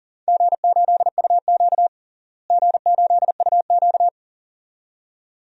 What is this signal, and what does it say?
Morse code, heard as a keyed tone, about 40 wpm, 700 Hz. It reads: G8UQ G8UQ
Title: DXCC-100 Call Signs - 4 Characters - Repeated 2x - 40wpm